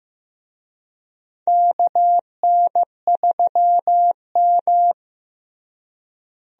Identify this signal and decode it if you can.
Morse code, heard as a keyed tone, about 16 wpm, 700 Hz. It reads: KN3M
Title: USA Call Signs - 4 Characters - Repeated 2x - 15wpm